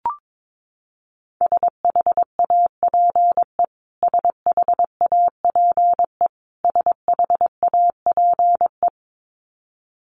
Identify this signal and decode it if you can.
Morse code, heard as a keyed tone, about 22 wpm, 700 Hz. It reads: SHAPE SHAPE SHAPE
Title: Top 500 Words Repeated 22wpm